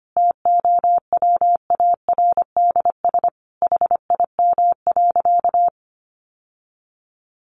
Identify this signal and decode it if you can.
Morse code, heard as a keyed tone, about 25 wpm, 700 Hz. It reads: TOWARDS HIM.
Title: Sentences from Top 1100 Words 25wpm